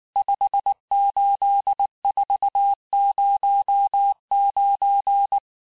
Morse code, heard as a keyed tone, about 19 wpm, 800 Hz. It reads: 58409
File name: cours-du-lundi files / cours-1595-teste-CHIFFRE-vitesse18-MPM